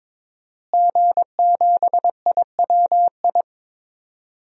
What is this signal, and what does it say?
Morse code, heard as a keyed tone, about 22 wpm, 700 Hz. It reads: G7IWI
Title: DXCC-100 Call Signs - 5 Characters - Repeated 3x - 22wpm